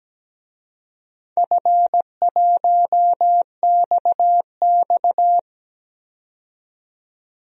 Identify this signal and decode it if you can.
Morse code, heard as a keyed tone, about 17 wpm, 700 Hz. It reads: F1XX
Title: DXCC-100 Call Signs - 4 Characters - 17wpm